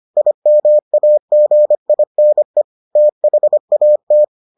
Morse code, heard as a keyed tone, about 25 wpm, 600 Hz. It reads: IMAGINE THAT